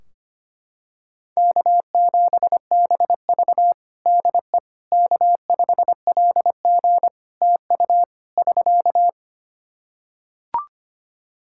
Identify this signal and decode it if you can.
Morse code, heard as a keyed tone, about 25 wpm, 700 Hz. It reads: K7BV DE K5LG TU <SK>
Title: K1USN SST QSO Practice 25wpm